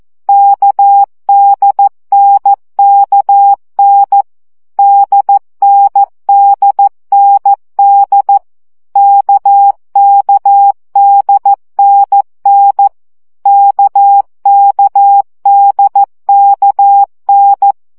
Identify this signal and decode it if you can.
Morse code, heard as a keyed tone, about 14 wpm, 800 Hz. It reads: KDNKN DNDND KKDNN KKDKN
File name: CW gyakorlas n d k 12 baud